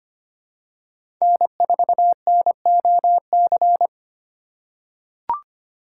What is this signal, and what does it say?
Morse code, heard as a keyed tone, about 25 wpm, 700 Hz. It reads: N4NOC
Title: USA Call Signs - 5 Characters - 6x Character Spacing - 25wpm